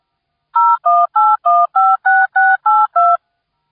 DTMF, heard as telephone touch tones, tones of about 220 ms. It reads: 018156682